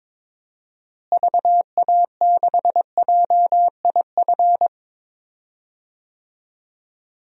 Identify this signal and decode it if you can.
Morse code, heard as a keyed tone, about 22 wpm, 700 Hz. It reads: VA6JIF
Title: DXCC-100 Call Signs - 6 Characters - Repeated 3x - 22wpm